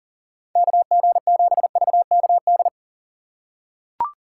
Morse code, heard as a keyed tone, about 38 wpm, 700 Hz. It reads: KG7VKD
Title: USA Call Signs - 2x Character Spacing - 40wpm